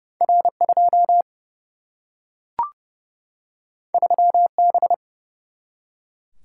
Morse code, heard as a keyed tone, about 30 wpm, 700 Hz. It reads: R2 3B
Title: Single-Letter Single-Number 30wpm